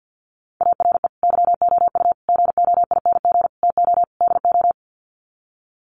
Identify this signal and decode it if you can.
Morse code, heard as a keyed tone, about 48 wpm, 700 Hz. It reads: ARE YOU GOING TO DO
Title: Sets of 5 Words 50wpm